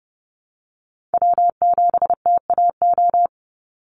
Morse code, heard as a keyed tone, about 30 wpm, 700 Hz. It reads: W7TAO
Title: NA SOTA Chaser Callsigns 30wpm